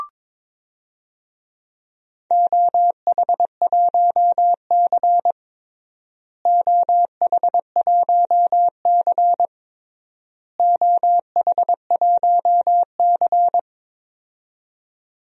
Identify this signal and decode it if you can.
Morse code, heard as a keyed tone, about 22 wpm, 700 Hz. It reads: OH1C OH1C OH1C
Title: DXCC-100 Call Signs - 4 Characters - Repeated 3x - 22wpm